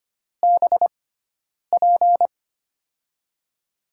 Morse code, heard as a keyed tone, about 25 wpm, spaced as for 9 wpm, 700 Hz. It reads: BP
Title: USA Call Signs - 6 Characters - 6x Character Spacing - 25wpm